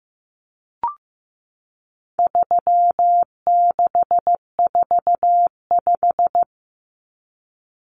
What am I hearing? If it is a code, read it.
Morse code, heard as a keyed tone, about 15 wpm, 700 Hz. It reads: 3645
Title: Common Cut Numbers - 4-Digit 15wpm